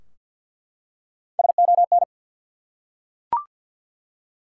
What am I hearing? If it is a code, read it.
Morse code, heard as a keyed tone, about 47 wpm, 700 Hz. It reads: SON
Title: Three Letter Words - 50wpm